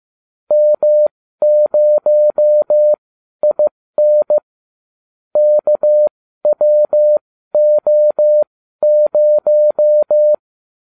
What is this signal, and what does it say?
Morse code, heard as a keyed tone, about 15 wpm, spaced as for 12 wpm, 600 Hz. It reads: M0IN KWO0